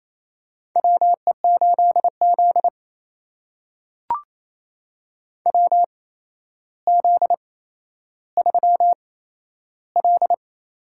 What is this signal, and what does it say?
Morse code, heard as a keyed tone, about 28 wpm, spaced as for 12 wpm, 700 Hz. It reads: WE8Z WZ3L
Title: USA Call Signs - 4 Characters - 8x Character Spacing - 28wpm